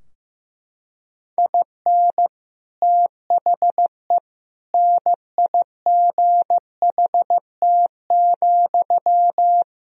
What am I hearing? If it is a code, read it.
Morse code, heard as a keyed tone, about 15 wpm, 700 Hz. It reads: IN THE NIGHT,
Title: Sentences from Top 1000 Words - Review Part 2 - 15wpm